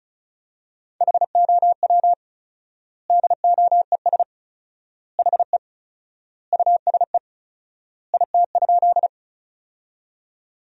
Morse code, heard as a keyed tone, about 36 wpm, 700 Hz. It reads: HOW DOES HE USE IT?
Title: 4x Word-Spacing - Sentences from Top 200 Words - 35wpm